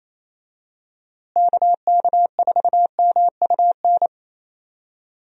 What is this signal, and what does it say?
Morse code, heard as a keyed tone, about 27 wpm, 700 Hz. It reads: KK4MUN
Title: USA Call Signs - 6 Characters - Repeated 2x - 28wpm